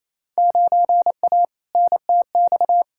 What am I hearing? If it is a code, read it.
Morse code, heard as a keyed tone, about 28 wpm, 700 Hz. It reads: 9A NTX
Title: Field Day Exchanges Encore - 28wpm